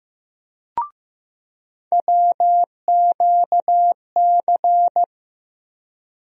Morse code, heard as a keyed tone, about 15 wpm, 700 Hz. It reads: WQC
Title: USA Call Sign Postfixes - 3 Letters 15wpm